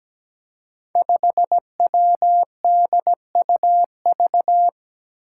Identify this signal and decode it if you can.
Morse code, heard as a keyed tone, about 17 wpm, 700 Hz. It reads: 5WDUV
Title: Five Challenging Letter-Number Combinations 17wpm